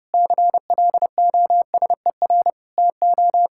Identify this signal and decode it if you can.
Morse code, heard as a keyed tone, about 28 wpm, 700 Hz. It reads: CLOSER TO
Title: Pangram Challenge v2 - 30wpm